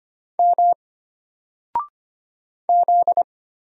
Morse code, heard as a keyed tone, about 25 wpm, 700 Hz. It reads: M Z
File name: Single-Letter-Number-25wpm